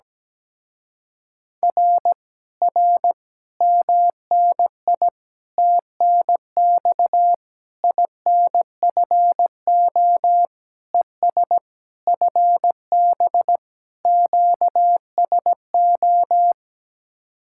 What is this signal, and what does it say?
Morse code, heard as a keyed tone, about 17 wpm, 700 Hz. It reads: R R MNI TNX INFO ES FB QSO
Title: Speed-Racing Common QSO Phrases Encore - 17wpm